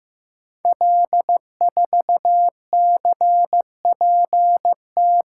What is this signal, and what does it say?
Morse code, heard as a keyed tone, about 15 wpm, 700 Hz. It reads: L4CPT
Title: Code Groups - 5 Letter:Numbers - 15wpm